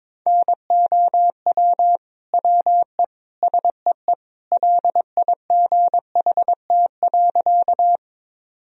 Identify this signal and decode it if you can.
Morse code, heard as a keyed tone, about 22 wpm, 700 Hz. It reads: NOW WE SEE LIGHT.
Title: Sentences from Top 1000 Words - Review Part 1 - 22wpm